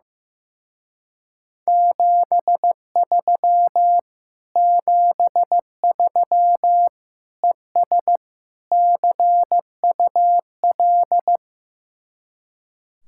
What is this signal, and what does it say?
Morse code, heard as a keyed tone, about 15 wpm, 700 Hz. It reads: 73 73 ES CUL
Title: Speed-Racing Common QSO Phrases Encore - 15wpm